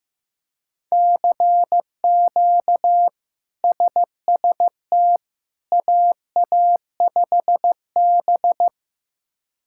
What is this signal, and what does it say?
Morse code, heard as a keyed tone, about 15 wpm, 700 Hz. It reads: CQ SST AA5B
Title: K1USN SST QSO Practice - 15wpm